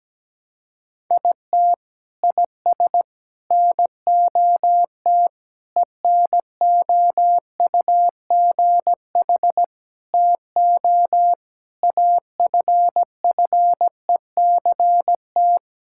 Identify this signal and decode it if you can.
Morse code, heard as a keyed tone, about 17 wpm, 700 Hz. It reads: IT IS NOT ENOUGH TO AFFECT
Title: Sentences from Top 1000 Words - Review Part 6 - 17wpm